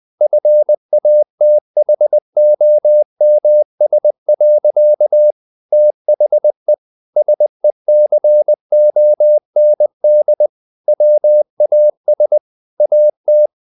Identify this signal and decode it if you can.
Morse code, heard as a keyed tone, about 20 wpm, 600 Hz. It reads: FATHOMS. THE SECOND WAS AT
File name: the old man and the sea / the old man and the sea part 1 20wpm